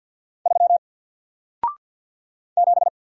Morse code, heard as a keyed tone, about 48 wpm, 700 Hz. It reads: 3 7
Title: Single Letter-Number Alt 50wpm